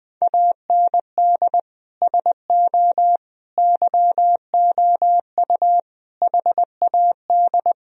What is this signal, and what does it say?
Morse code, heard as a keyed tone, about 20 wpm, 700 Hz. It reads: AND SO YOU HAD
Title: Sentences from Top 1000 Words - Review Part 1 - 20wpm